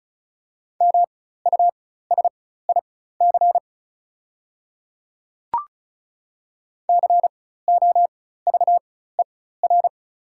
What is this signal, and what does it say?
Morse code, heard as a keyed tone, about 34 wpm, spaced as for 16 wpm, 700 Hz. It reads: MUSIC COVER
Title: 4x Character Spacing - Top 300 Words - 35wpm